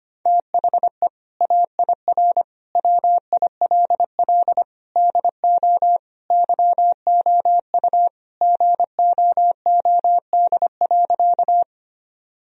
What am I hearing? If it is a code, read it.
Morse code, heard as a keyed tone, about 25 wpm, 700 Hz. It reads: THE AIR WILL DO YOU GOOD.
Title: Sentences from Top 1000 Words - Review Part 1 - 25wpm